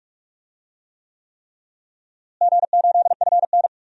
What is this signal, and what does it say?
Morse code, heard as a keyed tone, about 46 wpm, 700 Hz. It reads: G8FN
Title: DXCC-100 Call Signs - 4 Characters - Repeated 3x - 45wpm